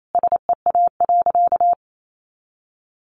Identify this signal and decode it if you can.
Morse code, heard as a keyed tone, about 28 wpm, 700 Hz. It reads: SEA.
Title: Sentences from Top 1300 Words 28wpm